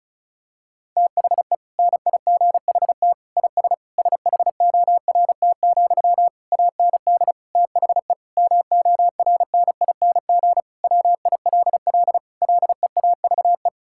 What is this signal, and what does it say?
Morse code, heard as a keyed tone, about 36 wpm, 700 Hz. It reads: THE NIGHT IS SHORT, AND THE MORNING WILL LEAVE
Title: Sentences from Top 1600 Words 35wpm